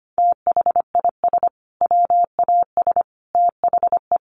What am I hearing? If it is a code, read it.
Morse code, heard as a keyed tone, about 25 wpm, 700 Hz. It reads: THIS WAS THE
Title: Sets of 5 Words Encore 25wpm